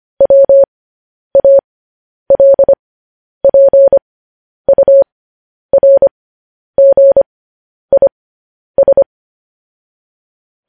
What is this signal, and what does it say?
Morse code, heard as a keyed tone, about 25 wpm, spaced as for 10 wpm, 550 Hz. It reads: WALPURGIS
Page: news-headlines-in-morse-code-at-10-wpm